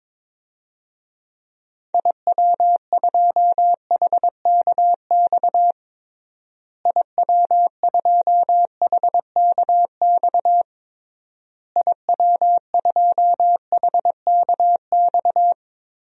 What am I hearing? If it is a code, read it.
Morse code, heard as a keyed tone, about 22 wpm, 700 Hz. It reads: IW2HKX IW2HKX IW2HKX
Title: DXCC-100 Call Signs - 6 Characters - Repeated 3x - 22wpm